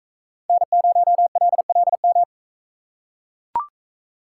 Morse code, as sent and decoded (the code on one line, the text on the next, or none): -. ----- .-.. .-.. --
N0LLM